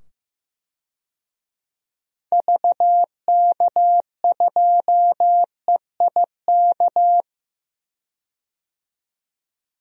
Morse code, heard as a keyed tone, about 15 wpm, 700 Hz. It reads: VK2EIK
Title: DXCC-100 Call Signs - 6 Characters - 15wpm